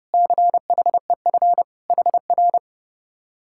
Morse code, heard as a keyed tone, about 33 wpm, 700 Hz. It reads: CHEF HR